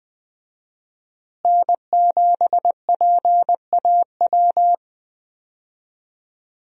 Morse code, heard as a keyed tone, about 20 wpm, 700 Hz. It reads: N7PAW
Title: USA Call Signs - 5 Characters - Repeated 3x - 20wpm